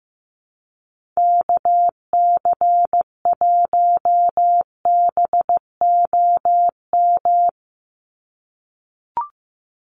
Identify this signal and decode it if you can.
Morse code, heard as a keyed tone, about 15 wpm, 700 Hz. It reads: KC1BOM
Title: USA Call Signs 15wpm